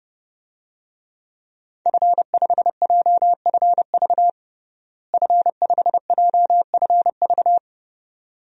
Morse code, heard as a keyed tone, about 32 wpm, 700 Hz. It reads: F5JFV F5JFV
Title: DXCC-100 Call Signs - 5 Characters - Repeated 2x - 30wpm